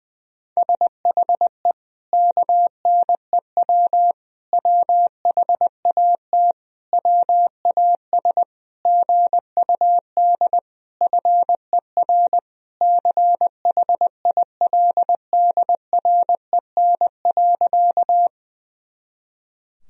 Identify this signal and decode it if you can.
Morse code, heard as a keyed tone, about 21 wpm, 700 Hz. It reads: SHE KNEW WHAT WAS GUD FER CHILDREN.
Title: Sentences from Top 400 Words and Abbreviations 20wpm